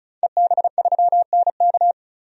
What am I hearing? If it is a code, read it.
Morse code, heard as a keyed tone, about 36 wpm, 700 Hz. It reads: EB3NK
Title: DXCC-100 Call Signs - 5 Characters - 35wpm